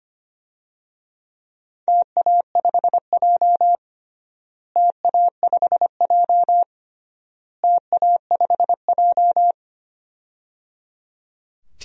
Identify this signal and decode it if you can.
Morse code, heard as a keyed tone, about 25 wpm, 700 Hz. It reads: TA5J TA5J TA5J
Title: DXCC-100 Call Signs - 4 Characters - Repeated 3x - 25wpm